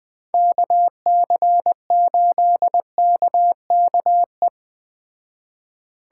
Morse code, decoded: KC8KKE

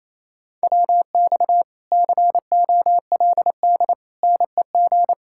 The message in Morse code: WX COLD NEG

